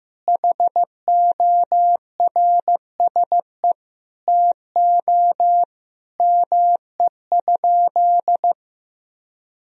Morse code, heard as a keyed tone, about 15 wpm, 700 Hz. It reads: HORSE TO ME?
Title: Sentences from Top 1000 Words - Review Part 3 - 15wpm